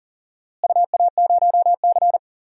Morse code, heard as a keyed tone, about 40 wpm, 700 Hz. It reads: UA0C
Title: DXCC-100 Call Signs - 4 Characters - Repeated 3x - 40wpm